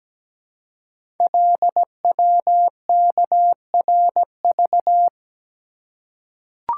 Morse code, heard as a keyed tone, about 17 wpm, 700 Hz. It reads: LWKRV
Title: Code Groups - 5 Letters - 17wpm